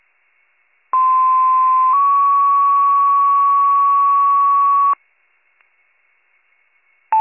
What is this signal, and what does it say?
Two-tone page, heard as a one-second tone, then a three-second tone, about 1040 then 1120 Hz. It is heard from 0.9 to 4.9 s.